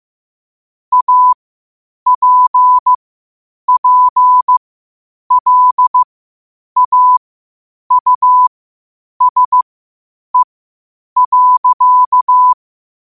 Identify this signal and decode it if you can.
Morse code, heard as a keyed tone, about 15 wpm, spaced as for 8 wpm, 1000 Hz. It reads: APPLAUSE.
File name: qotdfiles / 8wpm